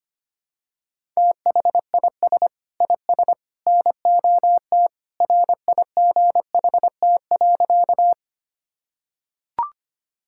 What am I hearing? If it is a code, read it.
Morse code, heard as a keyed tone, about 25 wpm, 700 Hz. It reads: THIS IS NOT RIGHT.
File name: practice / 25wpm / Sentences with 3-Word Phrases - 25wpm